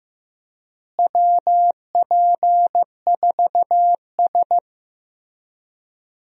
Morse code, heard as a keyed tone, about 15 wpm, 700 Hz. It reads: WP4S
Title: USA Call Signs - 4 Characters - Repeated 3x - 15wpm